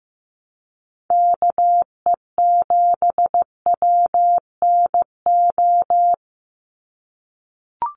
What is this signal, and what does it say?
Morse code, heard as a keyed tone, about 15 wpm, 700 Hz. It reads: KE7WNO
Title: USA Call Signs 15wpm